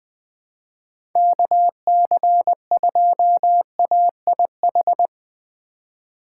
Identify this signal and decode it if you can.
Morse code, heard as a keyed tone, about 20 wpm, 700 Hz. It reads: KC2AIH
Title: USA Call Signs - 6 Characters - 4x Character Spacing - 20wpm